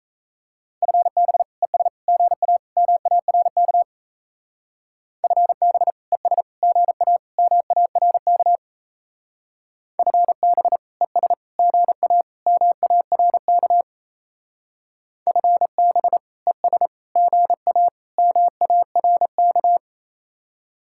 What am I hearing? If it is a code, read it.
Morse code, heard as a keyed tone, about 33 wpm, 700 Hz. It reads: FB ES GA MARK FB ES GA MARK FB ES GA MARK FB ES GA MARK
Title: Speed-Racing Common QSO Phrases - 28wpm